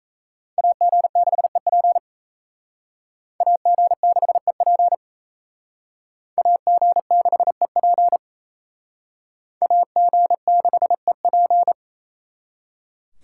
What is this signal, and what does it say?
Morse code, heard as a keyed tone, about 35 wpm, 700 Hz. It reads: AG6EP AG6EP AG6EP AG6EP